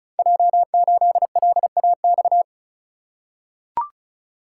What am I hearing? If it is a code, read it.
Morse code, heard as a keyed tone, about 36 wpm, 700 Hz. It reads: J8LAX
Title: Code Groups - 5 Letter:Numbers - 35wpm